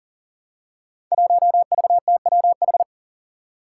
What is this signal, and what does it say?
Morse code, heard as a keyed tone, about 40 wpm, 700 Hz. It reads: 1VTWH
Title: Five Challenging Letter-Number Combinations 40wpm